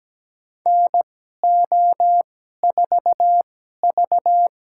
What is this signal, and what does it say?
Morse code, heard as a keyed tone, about 17 wpm, spaced as for 12 wpm, 700 Hz. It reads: NO4V
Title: USA Call Signs - 4 Characters - 2x Character Spacing - 17wpm